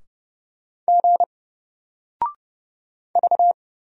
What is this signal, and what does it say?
Morse code, heard as a keyed tone, about 30 wpm, 700 Hz. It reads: G V
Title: Single Letter-Number Alt 30wpm